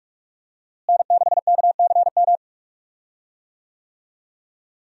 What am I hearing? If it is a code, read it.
Morse code, heard as a keyed tone, about 44 wpm, 700 Hz. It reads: N6KCM
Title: USA Call Signs - 5 Characters - Repeated 3x - 45wpm